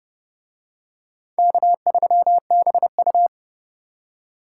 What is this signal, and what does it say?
Morse code, heard as a keyed tone, about 29 wpm, 700 Hz. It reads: K3BU